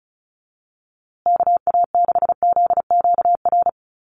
Morse code, heard as a keyed tone, about 34 wpm, 700 Hz. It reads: KA6ZQR
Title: USA Call Signs Repeated 3x 35wpm